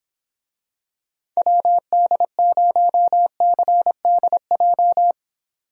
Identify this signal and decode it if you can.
Morse code, heard as a keyed tone, about 26 wpm, 700 Hz. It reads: WD0CDJ